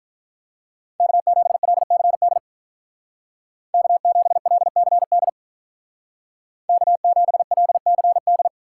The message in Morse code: K7LCD K7LCD K7LCD

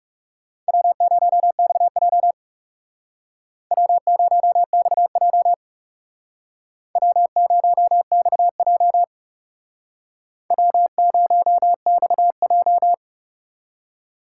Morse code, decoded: W0XJ W0XJ W0XJ W0XJ

